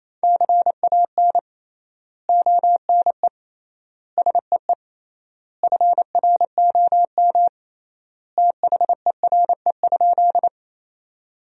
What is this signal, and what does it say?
Morse code, heard as a keyed tone, about 28 wpm, 700 Hz. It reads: CAN ONE SEE FROM THERE?
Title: 3x Word-Spacing - Sentences from Top 100 Words - 28wpm